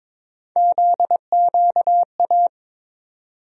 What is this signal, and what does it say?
Morse code, heard as a keyed tone, about 22 wpm, 700 Hz. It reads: ZQA
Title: USA Call Sign Postfixes - 3 Letters 22wpm